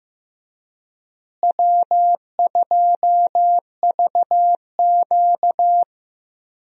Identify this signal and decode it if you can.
Morse code, heard as a keyed tone, about 15 wpm, 700 Hz. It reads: W2VQ
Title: USA Call Signs - 4 Characters - Repeated 3x - 15wpm